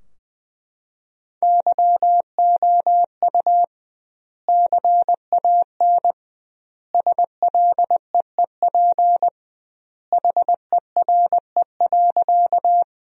Morse code, heard as a keyed tone, about 21 wpm, 700 Hz. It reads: YOU CAN SLEEP HERE.
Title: 2x Word-Spacing - Sentences from Top 500 Words - 20wpm